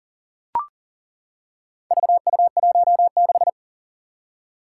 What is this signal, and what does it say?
Morse code, heard as a keyed tone, about 37 wpm, 700 Hz. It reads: VU16
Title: Four Challenging Letter-Number Combinations 40wpm